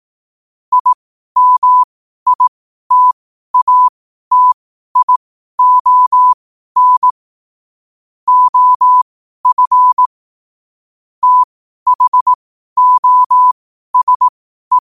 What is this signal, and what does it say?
Morse code, heard as a keyed tone, about 18 wpm, spaced as for 12 wpm, 1000 Hz. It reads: IMITATION OF THOSE
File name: qotdfiles / 12wpm